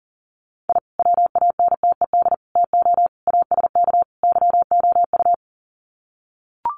Morse code, heard as a keyed tone, about 40 wpm, 700 Hz. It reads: I WANTED TO ASK YOU
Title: Sets of 5 Words Encore2 40wpm